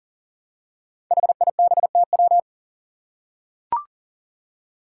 Morse code, heard as a keyed tone, about 39 wpm, 700 Hz. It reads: HIBTW
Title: Five Challenging Letter-Number Combinations 40wpm